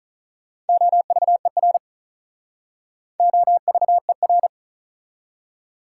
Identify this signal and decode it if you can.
Morse code, heard as a keyed tone, about 48 wpm, 700 Hz. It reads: OVER OVER